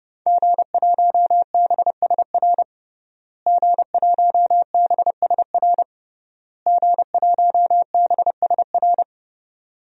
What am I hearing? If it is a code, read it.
Morse code, heard as a keyed tone, about 32 wpm, 700 Hz. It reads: G1BSR G1BSR G1BSR
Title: DXCC-100 Call Signs - 5 Characters - Repeated 3x - 30wpm